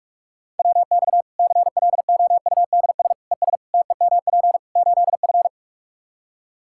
Morse code, heard as a keyed tone, about 46 wpm, 700 Hz. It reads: WX CLOUDS ES TEMP 8F
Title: Practice QSO - Exchange 2 - Info - 45wpm